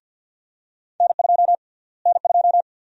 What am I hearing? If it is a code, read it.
Morse code, heard as a keyed tone, about 50 wpm, 700 Hz. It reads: N2 N2
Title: USA Call Sign Prefixes - Repeated 2x - 50wpm